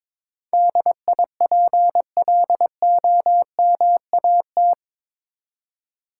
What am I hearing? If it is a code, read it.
Morse code, heard as a keyed tone, about 22 wpm, 700 Hz. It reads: DIPLOMAT
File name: practice / 22wpm / Eight Letter Words - 22wpm